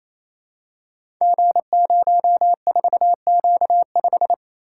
Morse code, heard as a keyed tone, about 27 wpm, 700 Hz. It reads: G04Q5
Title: Code Groups - 5 Letter:Numbers - 28wpm